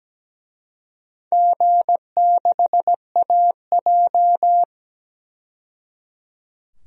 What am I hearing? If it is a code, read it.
Morse code, heard as a keyed tone, about 17 wpm, 700 Hz. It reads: G6AJ
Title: DXCC-100 Call Signs - 4 Characters - 17wpm